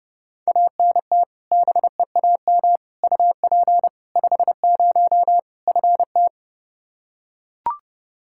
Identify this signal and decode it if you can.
Morse code, heard as a keyed tone, about 32 wpm, 700 Hz. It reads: ANT BEAM UP 50 FT
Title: Practice QSO - Exchange 2 - Info - 30wpm